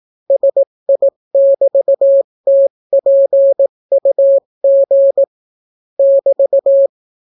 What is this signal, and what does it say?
Morse code, heard as a keyed tone, about 18 wpm, 550 Hz. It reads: SI<BT>TPUG <BT>